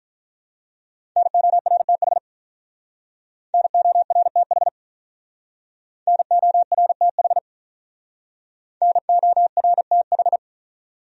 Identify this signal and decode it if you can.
Morse code, heard as a keyed tone, about 48 wpm, 700 Hz. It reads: NORTH NORTH NORTH NORTH